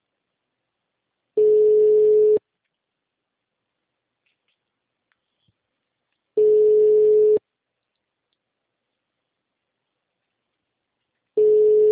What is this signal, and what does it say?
Ringback tone (Europe, 425 Hz).
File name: out- -2025- - - .2143